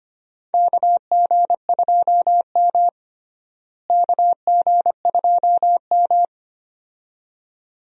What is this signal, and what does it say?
Morse code, heard as a keyed tone, about 25 wpm, 700 Hz. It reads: KG2M KG2M
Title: USA Call Signs - 4 Characters - Repeated 2x - 25wpm